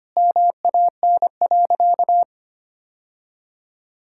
Morse code, heard as a keyed tone, about 25 wpm, 700 Hz. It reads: MAN.
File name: practice / 25wpm / Sentences with 3-Word Phrases - 25wpm